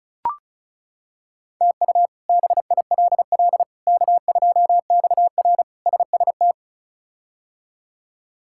Morse code, heard as a keyed tone, about 34 wpm, 700 Hz. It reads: TU BILL K2XR SST
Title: K1USN SST QSO Practice - 35wpm